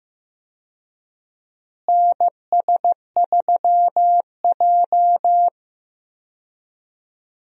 Morse code, heard as a keyed tone, about 15 wpm, 700 Hz. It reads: NS3J
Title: USA Call Signs - 4 Characters - Repeated 3x - 15wpm